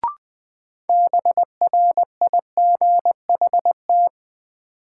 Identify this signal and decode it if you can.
Morse code, heard as a keyed tone, about 20 wpm, 700 Hz. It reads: BRIGHT
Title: Sentences from Top 1000 Words - New Word Drill - 20wpm